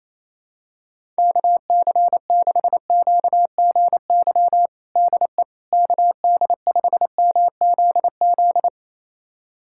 Morse code, heard as a keyed tone, about 28 wpm, 700 Hz. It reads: KC6QGY DE KD5MZZ
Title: USA Callsign DE Callsign - 6 Characters - 28wpm